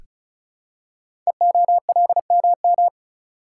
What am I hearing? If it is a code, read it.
Morse code, heard as a keyed tone, about 36 wpm, 700 Hz. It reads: EOLMM
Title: Code Groups - 5 Letters - 35wpm